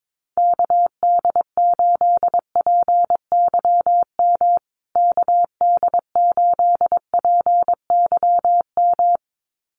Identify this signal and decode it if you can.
Morse code, heard as a keyed tone, about 22 wpm, 700 Hz. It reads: KD8PYM KD8PYM
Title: USA Call Signs Repeated 2x 22wpm